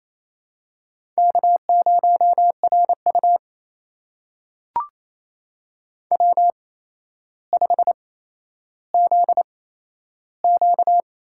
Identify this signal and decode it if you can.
Morse code, heard as a keyed tone, about 28 wpm, spaced as for 12 wpm, 700 Hz. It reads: K0RU W5ZQ
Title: USA Call Signs - 4 Characters - 8x Character Spacing - 28wpm